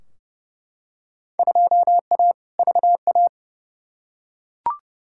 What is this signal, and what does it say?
Morse code, heard as a keyed tone, about 30 wpm, 700 Hz. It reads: 2A VA
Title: Field Day Exchanges - 30wpm